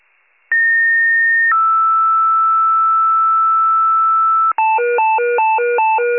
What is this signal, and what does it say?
Two-tone page, heard as a one-second tone, then a three-second tone, about 1850 then 1320 Hz, with noise. It is heard from 0.5 to 4.5 s.